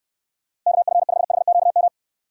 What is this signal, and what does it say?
Morse code, heard as a keyed tone, about 66 wpm, 700 Hz. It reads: KF5HQM